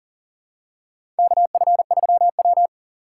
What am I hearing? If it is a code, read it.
Morse code, heard as a keyed tone, about 40 wpm, 700 Hz. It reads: KF3W
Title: USA Call Signs - 4 Characters - 2x Character Spacing - 40wpm